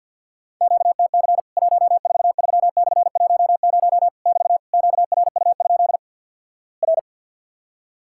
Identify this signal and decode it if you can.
Morse code, heard as a keyed tone, about 50 wpm, 700 Hz. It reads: QTC 143/10 <BT> QRU? R
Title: Worked All Europe DX Contest - QTC Exchange with Repeats - 50wpm